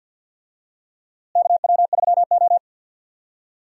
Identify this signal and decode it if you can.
Morse code, heard as a keyed tone, about 50 wpm, 700 Hz. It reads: KW3O
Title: USA Call Signs - 4 Characters - Repeated 3x - 50wpm